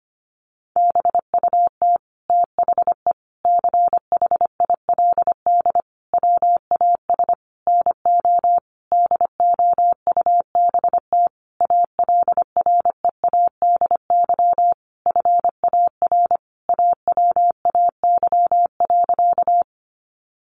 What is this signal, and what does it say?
Morse code, heard as a keyed tone, about 25 wpm, 700 Hz. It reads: BUT THE CHILD WAS NO DOUBT ALREADY FAR AWAY.
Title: Sentences from Top 1500 Words 25wpm